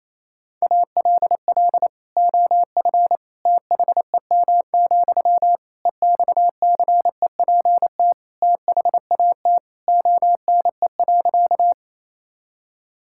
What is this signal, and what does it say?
Morse code, heard as a keyed tone, about 28 wpm, 700 Hz. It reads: ALL OF THEM, EXCEPT THAT ONE.
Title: Sentences from Top 1600 Words 28wpm